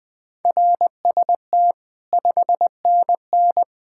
Morse code, heard as a keyed tone, about 19 wpm, 700 Hz. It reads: RST 5NN
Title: Speed-Racing Common QSO Phrases - 17wpm